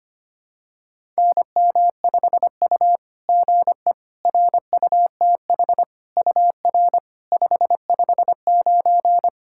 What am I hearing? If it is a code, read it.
Morse code, heard as a keyed tone, about 25 wpm, 700 Hz. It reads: NM5U GE RUTH UR 559
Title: SOTA QSO Practice 25wpm